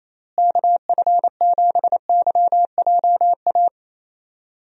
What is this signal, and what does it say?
Morse code, heard as a keyed tone, about 28 wpm, 700 Hz. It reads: KF7YJA
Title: USA Call Signs - 6 Characters - Repeated 3x - 28wpm